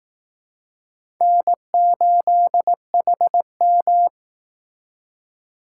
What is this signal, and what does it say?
Morse code, heard as a keyed tone, about 18 wpm, 700 Hz. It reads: N8HM
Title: Speed Racing - USA 1x2 Call Signs 15wpm